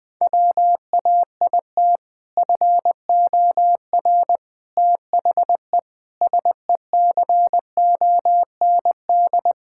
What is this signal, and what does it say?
Morse code, heard as a keyed tone, about 21 wpm, 700 Hz. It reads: WAIT FOR THE SECOND
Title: Sentences with 2-Word Phrases - 20wpm